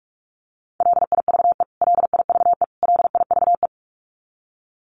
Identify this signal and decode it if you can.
Morse code, heard as a keyed tone, about 44 wpm, 700 Hz. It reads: LIVE LIVE LIVE
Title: Top 500 Words Repeated 45wpm